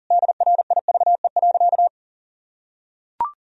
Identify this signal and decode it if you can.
Morse code, heard as a keyed tone, about 40 wpm, 700 Hz. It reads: DRIVE.
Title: Sentences from Top 1000 Words - Review Part 2 - 40wpm